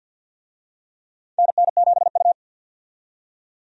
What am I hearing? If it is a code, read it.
Morse code, heard as a keyed tone, about 51 wpm, 700 Hz. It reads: NN7U
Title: USA Call Signs - 4 Characters - 50wpm